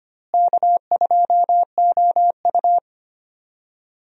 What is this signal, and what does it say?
Morse code, heard as a keyed tone, about 25 wpm, 700 Hz. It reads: K2OU
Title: USA Call Signs - 4 Characters - Repeated 2x - 25wpm